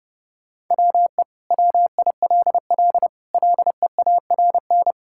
Morse code, heard as a keyed tone, about 30 wpm, 700 Hz. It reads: WE WILL LEARN
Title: Sentences from Top 1000 Words - Review Part 5 - 30wpm